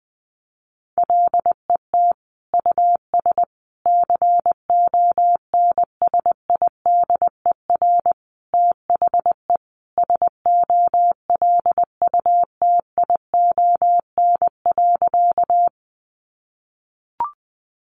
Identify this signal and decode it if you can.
Morse code, heard as a keyed tone, about 20 wpm, 700 Hz. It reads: LET US CONSIDER THE SOLUTION.
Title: Sentences from Top 1300 Words 20wpm